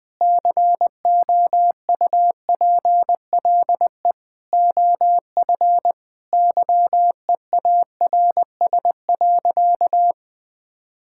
Morse code, heard as a keyed tone, about 21 wpm, 700 Hz. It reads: COUPLE OF YEARS.
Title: Sentences with 6-Word Phrases - 20wpm